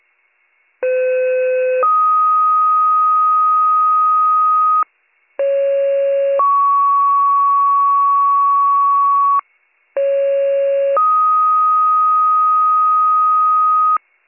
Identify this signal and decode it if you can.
Two-tone page, heard as a one-second tone, then a three-second tone, about 510 then 1250 Hz, about 550 then 1120 Hz, about 550 then 1250 Hz. It is heard from 0.8 to 14.0 s.